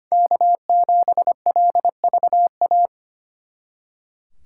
Morse code, heard as a keyed tone, about 31 wpm, 700 Hz. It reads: K7LVA